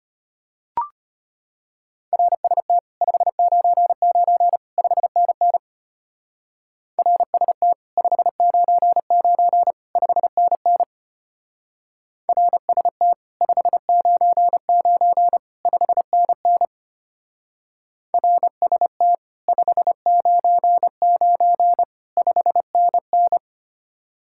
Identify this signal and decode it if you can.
Morse code, heard as a keyed tone, about 31 wpm, 700 Hz. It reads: RST 599 5NN RST 599 5NN RST 599 5NN RST 599 5NN
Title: Speed-Racing Common QSO Phrases Encore - 25wpm